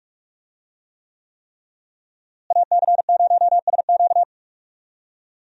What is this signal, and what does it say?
Morse code, heard as a keyed tone, about 43 wpm, 700 Hz. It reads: AC0SQ
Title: USA Call Signs - 5 Characters - Repeated 3x - 45wpm